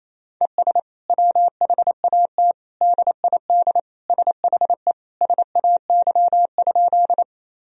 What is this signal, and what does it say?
Morse code, heard as a keyed tone, about 28 wpm, 700 Hz. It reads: ES WHAT DID SHE SAY?
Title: Sentences from Top 200 Words and Abbreviations 28wpm